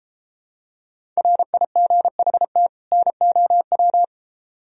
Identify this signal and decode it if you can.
Morse code, heard as a keyed tone, about 32 wpm, 700 Hz. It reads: RIGHT NOW